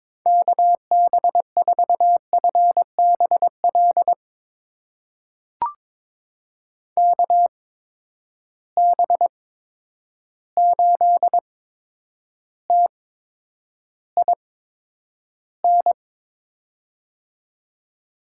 Morse code, heard as a keyed tone, about 22 wpm, spaced as for 10 wpm, 700 Hz. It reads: KB4FBL KB8TIN
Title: USA Call Signs - 6 Characters - 8x Character Spacing - 22wpm